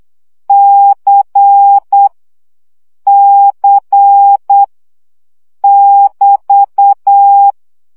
Morse code, heard as a keyed tone, about 8 wpm, 800 Hz. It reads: C C <BT>
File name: CW gyakorlas c 7 baud